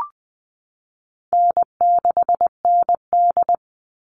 Morse code, heard as a keyed tone, about 20 wpm, 700 Hz. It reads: N6ND
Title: All Cut Numbers - 4-Digit 20wpm